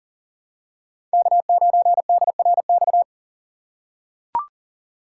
Morse code, heard as a keyed tone, about 40 wpm, 700 Hz. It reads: K9DRX
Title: USA Call Signs - 2x Character Spacing - 40wpm